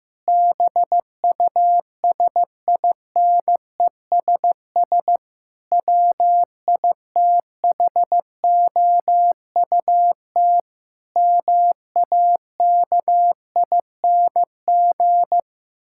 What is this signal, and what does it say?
Morse code, heard as a keyed tone, about 14 wpm, 700 Hz. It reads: BUSINESS WITHOUT MAKING
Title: Sentences from Top 1000 Words - Review Part 6 - 15wpm